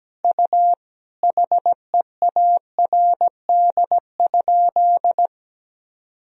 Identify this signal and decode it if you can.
Morse code, heard as a keyed tone, about 17 wpm, 700 Hz. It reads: U HEARD?
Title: Sentences from Top 400 Words and Abbreviations 17wpm